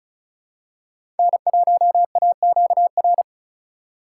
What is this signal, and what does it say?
Morse code, heard as a keyed tone, about 36 wpm, 700 Hz. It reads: N1AQR
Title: USA Call Signs - 5 Characters - 2x Character Spacing - 35wpm